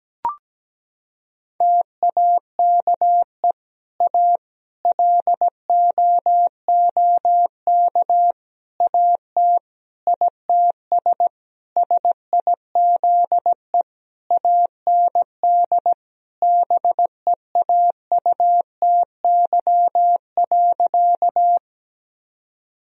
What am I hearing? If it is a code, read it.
Morse code, heard as a keyed tone, about 17 wpm, 700 Hz. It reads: TAKE A LOOK AT ITS SIZE AND BEAUTY.
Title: Sentences from Top 1000 Words - Review Part 2 - 17wpm